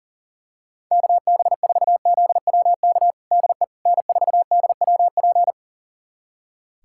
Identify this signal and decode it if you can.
Morse code, heard as a keyed tone, about 40 wpm, 700 Hz. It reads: KB4ZWK DE N4DWP
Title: USA Callsign DE Callsign 40wpm